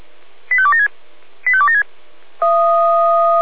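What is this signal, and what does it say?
Five-tone selective call (ZVEI), heard as quick runs of five tones, tones of about 70 ms. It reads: 85276 85276
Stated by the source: ZVEI five-tone selective call received by SDR, recorded by Feuerwehr Orlamünde.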